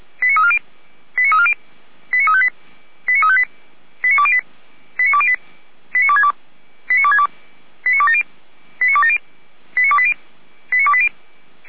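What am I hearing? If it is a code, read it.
Five-tone selective call (ZVEI), heard as quick runs of five tones, tones of about 70 ms. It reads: 89340 89340 89358 89358 89208 89208 89262 89262 89270 89270 89280 89280
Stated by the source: Five-tone ZVEI selective call received by SDR for Feuerwehr Orlamünde, address 89305.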